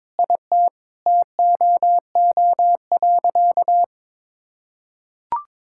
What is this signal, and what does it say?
Morse code, heard as a keyed tone, about 22 wpm, 700 Hz. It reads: IT TOO.
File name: Sentences with 9-Word Phrases - 22wpm